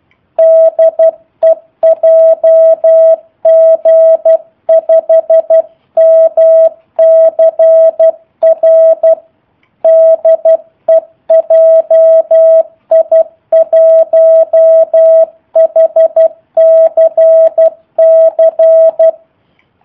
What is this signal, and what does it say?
Morse code, heard as a keyed tone, about 12 wpm, 650 Hz. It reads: DE JG5MCR DE JI1HCC
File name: jh8cbh / 2019cw / sp03- w